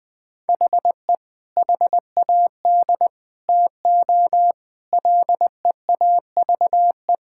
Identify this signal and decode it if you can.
Morse code, heard as a keyed tone, about 21 wpm, 700 Hz. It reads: HE HAD TO LEAVE
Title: Sentences with 3-Word Phrases - Spoken Beginning - 20wpm